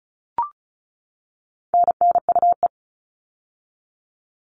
Morse code, heard as a keyed tone, about 34 wpm, 700 Hz. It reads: NNUE
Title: All Cut Numbers - 4-Digit 35wpm